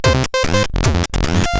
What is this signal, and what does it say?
There is no code.